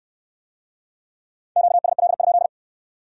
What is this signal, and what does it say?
Morse code, heard as a keyed tone, about 64 wpm, 700 Hz. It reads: QSL?